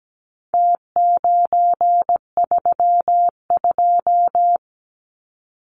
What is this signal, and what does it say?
Morse code, heard as a keyed tone, about 17 wpm, 700 Hz. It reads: T932
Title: Numbers - Contest Style - 4-Digit 17wpm